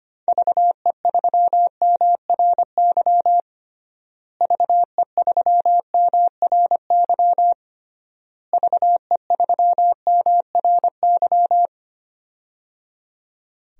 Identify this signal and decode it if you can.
Morse code, heard as a keyed tone, about 25 wpm, 700 Hz. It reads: VE3MRY VE3MRY VE3MRY
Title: DXCC-100 Call Signs - 6 Characters - Repeated 3x - 25wpm